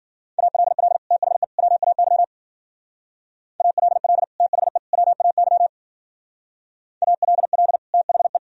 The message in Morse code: ALL THE WAY ALL THE WAY ALL THE

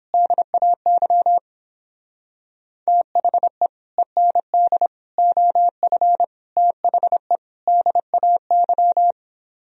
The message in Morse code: DAY THE END OF THE DAY

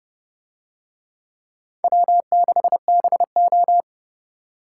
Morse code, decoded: W6BO

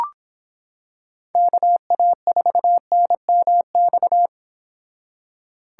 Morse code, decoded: KA4NMX

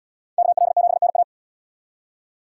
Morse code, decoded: KL7NA

